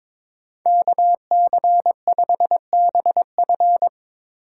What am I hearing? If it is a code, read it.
Morse code, heard as a keyed tone, about 25 wpm, 700 Hz. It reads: KC5BF
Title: Speed Racing - USA 2x2 Call Signs 22wpm